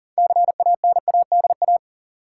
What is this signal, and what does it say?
Morse code, heard as a keyed tone, about 40 wpm, 700 Hz. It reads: CANADA
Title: Sets of 5 Words - New Word Drill - 40wpm